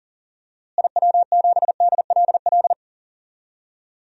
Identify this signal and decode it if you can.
Morse code, heard as a keyed tone, about 40 wpm, 700 Hz. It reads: IW7DLL
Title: DXCC-100 Call Signs - 6 Characters - Repeated 3x - 40wpm